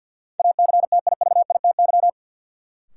Morse code, heard as a keyed tone, about 53 wpm, 700 Hz. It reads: ACTIVITY
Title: Sentences from Top 1000 Words - Review Part 4 - New Word Drill - 50wpm